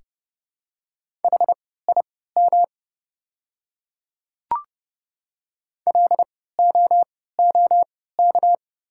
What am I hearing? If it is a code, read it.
Morse code, heard as a keyed tone, about 31 wpm, spaced as for 17 wpm, 700 Hz. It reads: HIM LOOK
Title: 3x Character Spacing - Top 100 Words - 30wpm